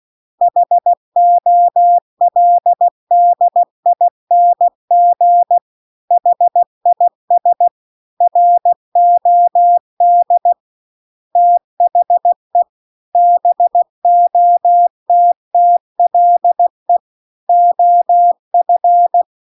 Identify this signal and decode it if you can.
Morse code, heard as a keyed tone, about 16 wpm, 700 Hz. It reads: HOLDING HIS ROD THE BOTTLE OF
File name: CW-The-Big-Twoheaded-River-2-16WPM